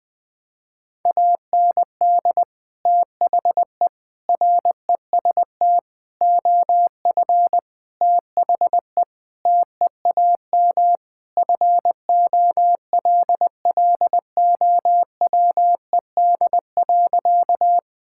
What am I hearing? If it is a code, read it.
Morse code, heard as a keyed tone, about 20 wpm, 700 Hz. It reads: AND THE REST OF THE TEAM FOLLOWED.
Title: Sentences with 5-Word Phrases - 20wpm